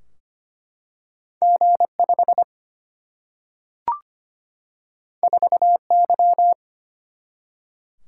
Morse code, heard as a keyed tone, about 25 wpm, 700 Hz. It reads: G5 4Y
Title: Single-Letter Single-Number 25wpm